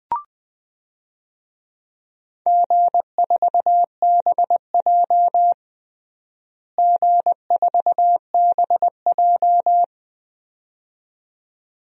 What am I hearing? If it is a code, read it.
Morse code, heard as a keyed tone, about 20 wpm, 700 Hz. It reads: G4BJ G4BJ
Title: DXCC-100 Call Signs - 4 Characters - Repeated 2x - 20wpm